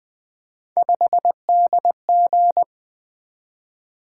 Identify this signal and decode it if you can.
Morse code, heard as a keyed tone, about 20 wpm, 700 Hz. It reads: 5DG
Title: Three Letter-Number Combinations - 20wpm